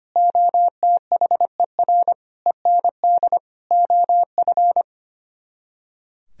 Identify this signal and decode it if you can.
Morse code, heard as a keyed tone, about 30 wpm, 700 Hz. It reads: OTHER END OF
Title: Speed-Racing - Sets of 5 Words 25wpm